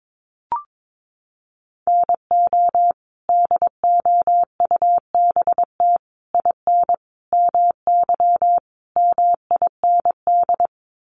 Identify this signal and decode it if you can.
Morse code, heard as a keyed tone, about 22 wpm, 700 Hz. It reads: NO DOUBT IN MY MIND
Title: Sets of 5 Words Encore2 22wpm